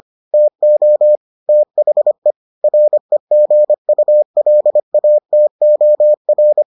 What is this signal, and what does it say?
Morse code, heard as a keyed tone, about 25 wpm, 600 Hz. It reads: TO THE REGULATOR